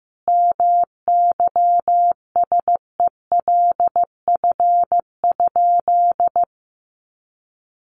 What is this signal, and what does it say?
Morse code, heard as a keyed tone, about 14 wpm, 700 Hz. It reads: MYSELF?
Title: Sentences from Top 1300 Words 15wpm